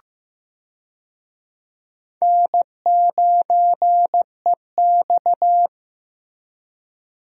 Morse code, heard as a keyed tone, about 15 wpm, 700 Hz. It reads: N9EX
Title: USA Call Signs - 4 Characters - Repeated 3x - 15wpm